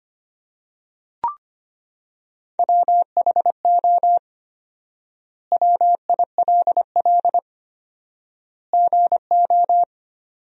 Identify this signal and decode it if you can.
Morse code, heard as a keyed tone, about 25 wpm, 700 Hz. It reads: WHO WILL GO
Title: 4x Word-Spacing - Sentences from Top 100 Words - 25wpm